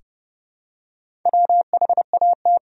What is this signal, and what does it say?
Morse code, heard as a keyed tone, about 30 wpm, 700 Hz. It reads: WHAT
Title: Sentences from Top 100 Words and Abbreviations 30wpm